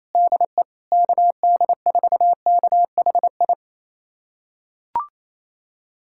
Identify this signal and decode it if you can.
Morse code, heard as a keyed tone, about 28 wpm, 700 Hz. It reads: DE KD4KHI
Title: USA Callsign DE Callsign - 6 Characters - 28wpm